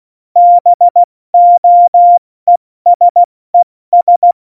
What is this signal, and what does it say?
Morse code, heard as a keyed tone, about 16 wpm, spaced as for 14 wpm, 700 Hz. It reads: BOESES
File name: Morsekurs / Witze / Witz 28 April 2020, 70 F80